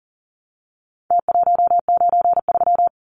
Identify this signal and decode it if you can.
Morse code, heard as a keyed tone, about 39 wpm, 700 Hz. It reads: T193
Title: Numbers - Contest Style - 4-Digit 40wpm